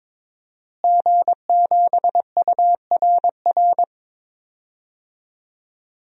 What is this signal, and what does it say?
Morse code, heard as a keyed tone, about 22 wpm, 700 Hz. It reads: G7URR
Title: DXCC-100 Call Signs - 5 Characters - Repeated 3x - 22wpm